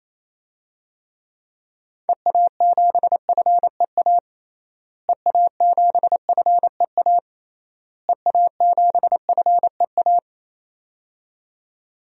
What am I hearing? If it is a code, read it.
Morse code, heard as a keyed tone, about 28 wpm, 700 Hz. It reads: EA7FEA EA7FEA EA7FEA
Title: DXCC-100 Call Signs - 6 Characters - Repeated 3x - 28wpm